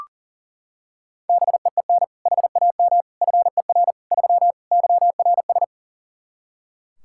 Morse code, heard as a keyed tone, about 40 wpm, 700 Hz. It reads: BEEN HAM FER 3 YRS
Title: Practice QSO - Exchange 3 - More Info - 40wpm